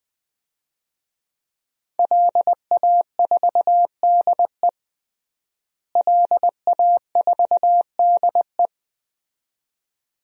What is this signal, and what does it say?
Morse code, heard as a keyed tone, about 20 wpm, 700 Hz. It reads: LA4DE LA4DE
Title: DXCC-100 Call Signs - 5 Characters - Repeated 2x - 20wpm